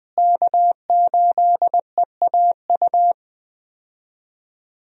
Morse code, heard as a keyed tone, about 20 wpm, 700 Hz. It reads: K8EAU
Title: USA Call Signs - 5 Characters - Repeated 2x - 20wpm